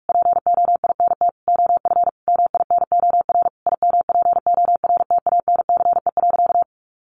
Morse code, heard as a keyed tone, about 44 wpm, 700 Hz. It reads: POINT OF MINOR IMPORTANCE.
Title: Sentences from Top 1500 Words 45wpm